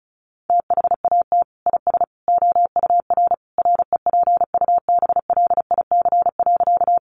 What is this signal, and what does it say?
Morse code, heard as a keyed tone, about 34 wpm, 700 Hz. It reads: THAT IS OUR REPUBLIC.
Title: Sentences from Top 1500 Words 35wpm